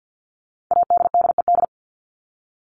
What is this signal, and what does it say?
Morse code, heard as a keyed tone, about 50 wpm, 700 Hz. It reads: ADDED
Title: ED Suffix - Top 100 Words with Base Words 50wpm